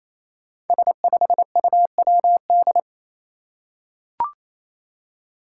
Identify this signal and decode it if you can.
Morse code, heard as a keyed tone, about 27 wpm, 700 Hz. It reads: S5UWD
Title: Five Challenging Letter-Number Combinations 28wpm